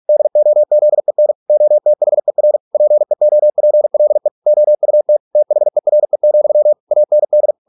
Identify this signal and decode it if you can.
Morse code, heard as a keyed tone, about 46 wpm, 600 Hz. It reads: DOZEN OTHER PEOPLE OUT THERE, AND